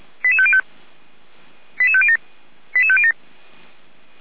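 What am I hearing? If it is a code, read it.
Five-tone selective call (ZVEI), heard as quick runs of five tones, tones of about 70 ms. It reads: 88505 88507 88507
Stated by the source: ZVEI five-tone selective call received on an SDR.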